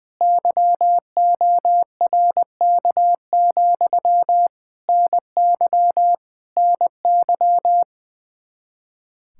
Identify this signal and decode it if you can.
Morse code, heard as a keyed tone, about 20 wpm, 700 Hz. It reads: YORK, NY NY
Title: K1USN SST QSO Practice 20wpm